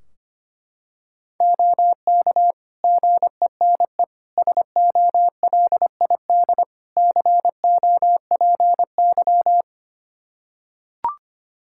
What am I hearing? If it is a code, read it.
Morse code, heard as a keyed tone, about 25 wpm, 700 Hz. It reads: OK GENE SOLID COPY
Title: Practice QSO - Exchange 3 - More Info - 25wpm